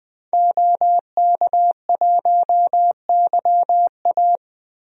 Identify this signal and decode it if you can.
Morse code, heard as a keyed tone, about 21 wpm, 700 Hz. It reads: OK1YA
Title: DXCC-100 Call Signs - 5 Characters - Repeated 2x - 20wpm